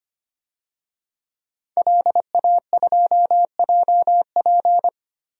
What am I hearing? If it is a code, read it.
Morse code, heard as a keyed tone, about 25 wpm, 700 Hz. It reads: LA2JP
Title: DXCC-100 Call Signs - 5 Characters - 25wpm